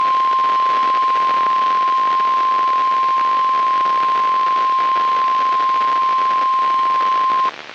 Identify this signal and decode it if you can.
Emergency Alert System (SAME) tones (the 1050 Hz attention tone), with noise. It is heard from 0.8 to 7.1 s.
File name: eas messages / EAS WXL38 Binghamton